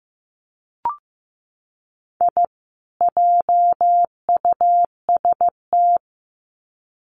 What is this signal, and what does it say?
Morse code, heard as a keyed tone, about 15 wpm, 700 Hz. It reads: I JUST
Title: Sets of 2 Words 15wpm